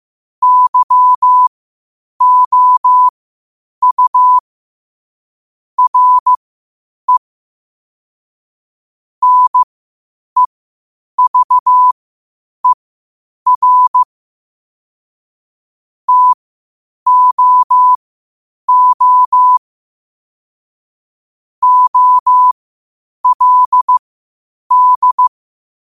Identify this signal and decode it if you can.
Morse code, heard as a keyed tone, about 15 wpm, spaced as for 8 wpm, 1000 Hz. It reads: YOU RE NEVER TOO OLD
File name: qotdfiles / 8wpm